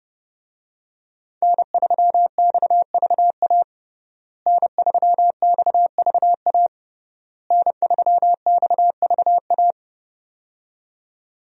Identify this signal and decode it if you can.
Morse code, heard as a keyed tone, about 30 wpm, 700 Hz. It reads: N3XVA N3XVA N3XVA
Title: USA Call Signs - 5 Characters - Repeated 3x - 30wpm